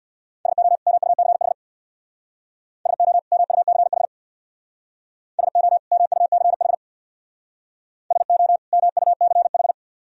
Morse code, heard as a keyed tone, about 59 wpm, 700 Hz. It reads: SO MUCH SO MUCH SO MUCH SO MUCH
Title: Speed-Racing - Sets of 2 Words 50wpm